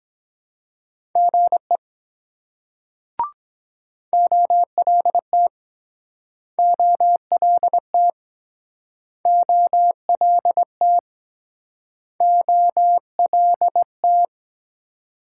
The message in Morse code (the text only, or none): GE OLT OLT OLT OLT